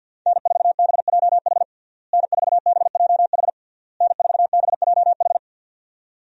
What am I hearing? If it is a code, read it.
Morse code, heard as a keyed tone, about 50 wpm, 700 Hz. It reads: N4BJH N4BJH N4BJH
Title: USA Call Signs - 5 Characters - Repeated 3x - 50wpm